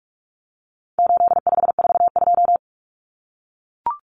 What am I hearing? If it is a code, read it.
Morse code, heard as a keyed tone, about 44 wpm, 700 Hz. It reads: 8542
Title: Common Cut Numbers - 4-Digit 45wpm